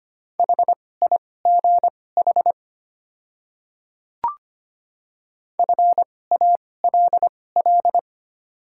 Morse code, heard as a keyed tone, about 25 wpm, spaced as for 18 wpm, 700 Hz. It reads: HIGH FALL
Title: 2x Character Spacing - Top 500 Words - 25wpm